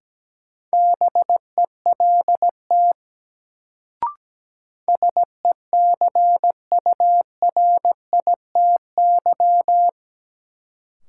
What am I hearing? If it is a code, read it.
Morse code, heard as a keyed tone, about 17 wpm, 700 Hz. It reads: BELT SECURITY
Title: Sentences from Top 1400 Words - New Word Drill Alt - 17wpm